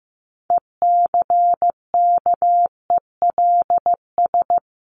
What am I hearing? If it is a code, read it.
Morse code, heard as a keyed tone, about 16 wpm, 700 Hz. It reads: ECKELS
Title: A Sound Of Thunder 15wpm